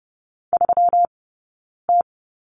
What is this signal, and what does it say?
Morse code, heard as a keyed tone, about 30 wpm, 700 Hz. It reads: 3 T
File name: Lesson 20d - Number 3 - Two Characters 30wpm